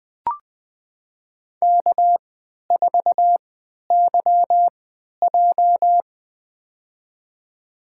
Morse code, heard as a keyed tone, about 20 wpm, spaced as for 11 wpm, 700 Hz. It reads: K4YJ
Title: USA Call Signs - 4 Characters - 3x Character Spacing - 20wpm